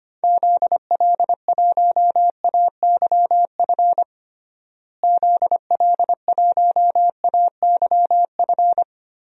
Morse code, heard as a keyed tone, about 25 wpm, 700 Hz. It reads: ZL1AYF ZL1AYF
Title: DXCC-100 Call Signs - 6 Characters - Repeated 3x - 25wpm